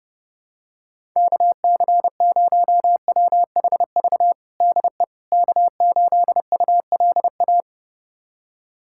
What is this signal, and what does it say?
Morse code, heard as a keyed tone, about 30 wpm, 700 Hz. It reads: KC0WHV DE K8ULA
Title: USA Callsign DE Callsign 30wpm